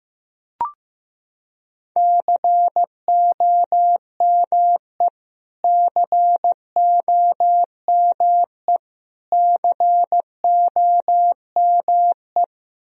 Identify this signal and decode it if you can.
Morse code, heard as a keyed tone, about 16 wpm, 700 Hz. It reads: COME COME COME
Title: Top 100 Words Repeated 15wpm